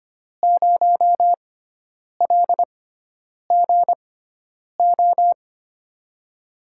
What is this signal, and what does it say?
Morse code, heard as a keyed tone, about 25 wpm, spaced as for 9 wpm, 700 Hz. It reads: 0LGO
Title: USA Call Signs - 6 Characters - 6x Character Spacing - 25wpm